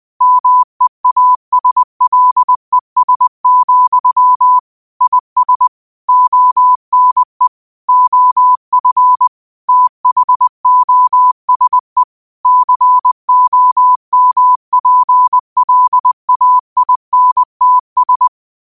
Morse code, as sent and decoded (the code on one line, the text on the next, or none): -- . .- ... .-.. ... --..-- / .. ... / --- -. / --- ..-. / - .... --- ... / -.-. --- -- .--. .-.. .- .. -. - ...
MEASLES, IS ONE OF THOSE COMPLAINTS